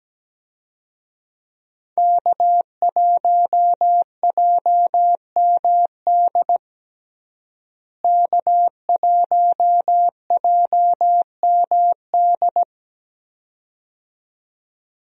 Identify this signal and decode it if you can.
Morse code, heard as a keyed tone, about 17 wpm, 700 Hz. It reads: K1JMD K1JMD
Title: USA Call Signs - 5 Characters - Repeated 2x - 17wpm